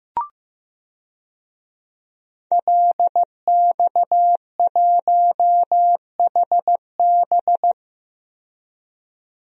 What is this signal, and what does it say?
Morse code, heard as a keyed tone, about 15 wpm, 700 Hz. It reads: LX1HB
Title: DXCC-100 Call Signs - 5 Characters - 15wpm